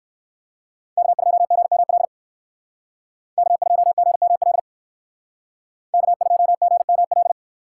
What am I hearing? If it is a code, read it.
Morse code, heard as a keyed tone, about 62 wpm, 700 Hz. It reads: K2GML K2GML K2GML